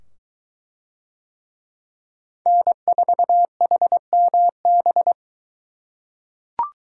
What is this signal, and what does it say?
Morse code, heard as a keyed tone, about 21 wpm, 700 Hz. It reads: N4HMB